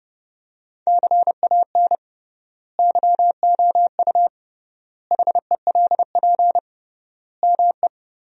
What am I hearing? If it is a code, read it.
Morse code, heard as a keyed tone, about 30 wpm, 700 Hz. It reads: CAN YOU HELP ME
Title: 3x Word-Spacing - Sentences from Top 200 Words - 30wpm